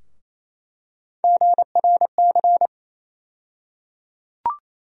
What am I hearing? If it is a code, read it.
Morse code, heard as a keyed tone, about 28 wpm, 700 Hz. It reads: GRC
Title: USA Call Sign Postfixes - 3 Letters 28wpm